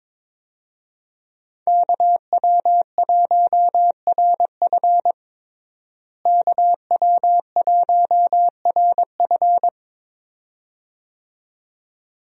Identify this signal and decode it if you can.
Morse code, heard as a keyed tone, about 22 wpm, 700 Hz. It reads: KW1RF KW1RF
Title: USA Call Signs - 5 Characters - Repeated 2x - 22wpm